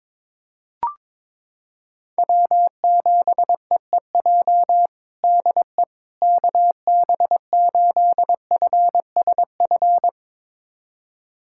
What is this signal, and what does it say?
Morse code, heard as a keyed tone, about 22 wpm, 700 Hz. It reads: W7EEJ DE KB8FSF
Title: USA Callsign DE Callsign 22wpm